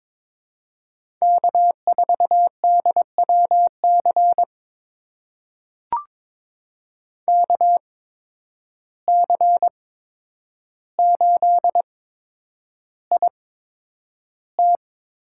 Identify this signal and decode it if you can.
Morse code, heard as a keyed tone, about 22 wpm, spaced as for 9 wpm, 700 Hz. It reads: K4DWC KC8IT
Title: USA Call Signs - 8x Character Spacing - 22wpm